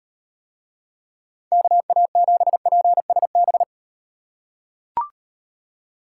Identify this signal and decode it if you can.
Morse code, heard as a keyed tone, about 38 wpm, 700 Hz. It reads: KA7PSB